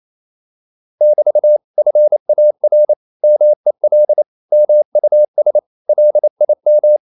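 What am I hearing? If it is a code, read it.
Morse code, heard as a keyed tone, about 28 wpm, 600 Hz. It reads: <BT> FAR MEL MUS LIM
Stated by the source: Danish short words sent in Morse at 28 wpm, from CW med Gnister.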